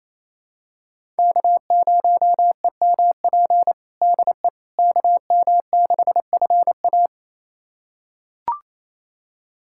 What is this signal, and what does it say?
Morse code, heard as a keyed tone, about 27 wpm, 700 Hz. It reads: K0EMP DE KM6FA
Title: USA Callsign DE Callsign - 5 Characters - 28wpm